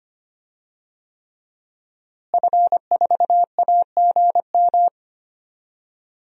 Morse code, heard as a keyed tone, about 25 wpm, 700 Hz. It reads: F4AGM
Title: DXCC-100 Call Signs - 5 Characters - Repeated 3x - 25wpm